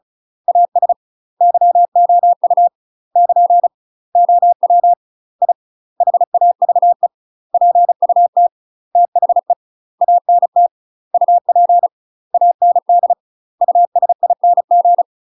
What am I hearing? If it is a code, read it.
Morse code, heard as a keyed tone, about 36 wpm, 700 Hz. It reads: AS YOU <KN> OW I HAVE PUT THE ANT UP AND USING
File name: UK MASTER QSOs TRAINING / 35WPM / UK MASTER QSOs 0015